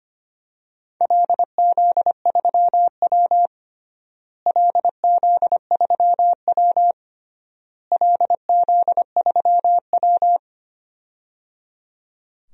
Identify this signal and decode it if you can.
Morse code, heard as a keyed tone, about 25 wpm, 700 Hz. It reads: LZ3W LZ3W LZ3W
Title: DXCC-100 Call Signs - 4 Characters - Repeated 3x - 25wpm